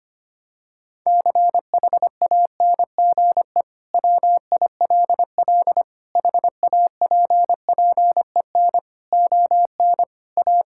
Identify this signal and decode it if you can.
Morse code, heard as a keyed tone, about 25 wpm, 700 Hz. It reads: CHANGE WILL HAPPEN ON A
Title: Sentences from Top 1000 Words - Review Part 5 - 25wpm